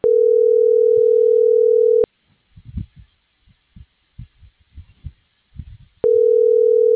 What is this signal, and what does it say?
Ringback tone (North America), 440 + 480 Hz.
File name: CENTRAL MAERSK / out adolfo.msk - .74577